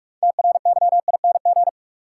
Morse code, heard as a keyed tone, about 42 wpm, 700 Hz. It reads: TRYING
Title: Sets of 2 Words - New Word Drill - 45wpm